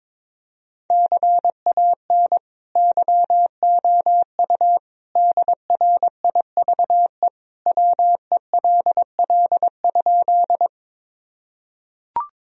Morse code, heard as a keyed tone, about 22 wpm, 700 Hz. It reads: CAN YOU DRIVE WELL?
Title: Sentences from Top 1000 Words - Review Part 2 - 22wpm